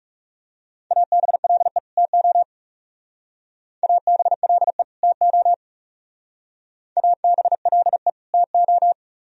Morse code, heard as a keyed tone, about 35 wpm, 700 Hz. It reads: ABLE TO ABLE TO ABLE TO
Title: Speed-Racing - Sets of 2 Words 30wpm